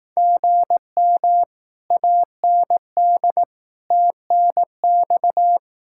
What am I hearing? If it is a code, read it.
Morse code, heard as a keyed tone, about 18 wpm, 700 Hz. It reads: GM AND TNX
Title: Speed-Racing Common QSO Phrases Encore - 15wpm